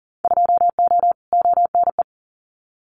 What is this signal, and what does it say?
Morse code, heard as a keyed tone, about 40 wpm, 700 Hz. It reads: 2O ONE
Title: Winter Field Day Exchanges Encore 40wpm